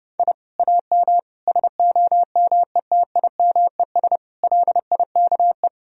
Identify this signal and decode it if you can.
Morse code, heard as a keyed tone, about 29 wpm, 700 Hz. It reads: I AM SOMETIMES LIKE
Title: Sentences from Top 1000 Words - Review Part 3 - 30wpm